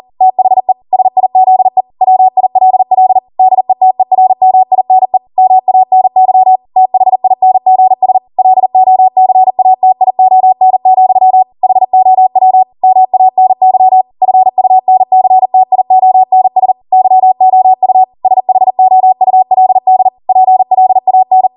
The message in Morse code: THE SIZE WILL DETERMINE MANY THINGS LOCATION, HOW MANY FUNCTIONS YOU SHOULD PLAN